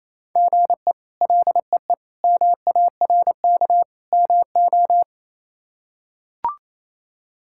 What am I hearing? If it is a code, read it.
Morse code, heard as a keyed tone, about 27 wpm, 700 Hz. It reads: GE LEE MARK MO
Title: K1USN SST QSO Practice - 28wpm